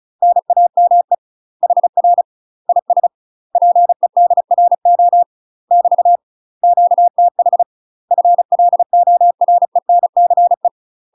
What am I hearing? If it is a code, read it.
Morse code, heard as a keyed tone, about 35 wpm, 700 Hz. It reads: NAME HR IS PEDRO <BT> QTH FLORENCE
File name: UK MASTER QSOs TRAINING / 35WPM / UK MASTER QSOs 0040